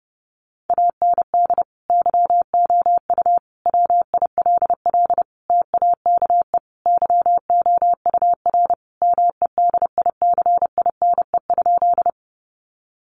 Morse code, heard as a keyed tone, about 30 wpm, 700 Hz. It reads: AND YOU WILL TAKE YOUR MEDICINE?
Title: Sentences from Top 900 Words 30wpm